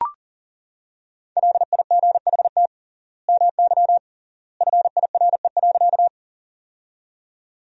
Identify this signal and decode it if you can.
Morse code, heard as a keyed tone, about 40 wpm, 700 Hz. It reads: LIGHT MY FIRE.
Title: 3x Word-Spacing - Sentences from Top 400 Words - 40wpm